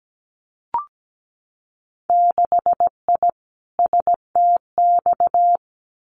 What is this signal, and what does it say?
Morse code, heard as a keyed tone, about 17 wpm, 700 Hz. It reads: 6I STX
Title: Winter Field Day Exchanges Encore 17wpm